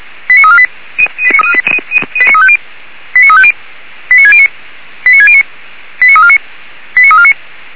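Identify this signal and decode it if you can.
Five-tone selective call (ZVEI), heard as quick runs of five tones, tones of about 70 ms. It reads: 89348 89350 89609 89609 89349 89349
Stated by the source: ZVEI five-tone selective call 89347 received on an SDR.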